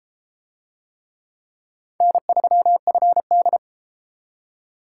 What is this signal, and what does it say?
Morse code, heard as a keyed tone, about 33 wpm, 700 Hz. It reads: N3FD